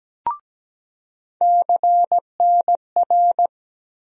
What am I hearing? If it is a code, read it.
Morse code, heard as a keyed tone, about 17 wpm, 700 Hz. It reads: CNR
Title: USA Call Sign Postfixes - 3 Letters 17wpm